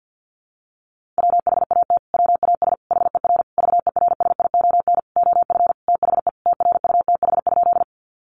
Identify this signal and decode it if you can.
Morse code, heard as a keyed tone, about 48 wpm, 700 Hz. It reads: WHAT WAS HER VERSION OF THE TRUTH?
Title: Sentences from Top 1400 Words 50wpm